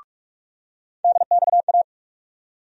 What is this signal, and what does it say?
Morse code, heard as a keyed tone, about 47 wpm, 700 Hz. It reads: DXA
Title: USA Call Sign Postfixes - 3 Letters 45wpm